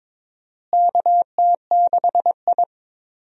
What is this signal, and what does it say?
Morse code, heard as a keyed tone, about 22 wpm, 700 Hz. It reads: KT6I
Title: USA Call Signs - 4 Characters - Repeated 3x - 22wpm